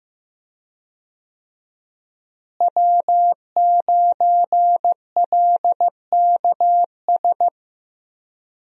Morse code, heard as a keyed tone, about 15 wpm, 700 Hz. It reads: W9LKS
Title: USA Call Signs - 5 Characters - Repeated 2x - 15wpm